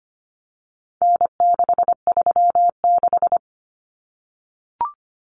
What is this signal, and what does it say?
Morse code, heard as a keyed tone, about 25 wpm, 700 Hz. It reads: N636
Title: Common Cut Numbers - 4-Digit 25wpm